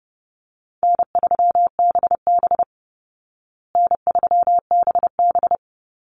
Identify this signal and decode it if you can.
Morse code, heard as a keyed tone, about 30 wpm, 700 Hz. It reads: N3BB N3BB
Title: Easy Call Signs Repeated 2x - 30wpm